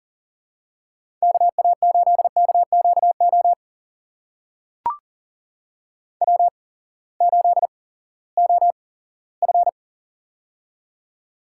Morse code, heard as a keyed tone, about 40 wpm, spaced as for 20 wpm, 700 Hz. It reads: KA8KQO W8OF
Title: USA Call Signs - 8x Character Spacing - 40wpm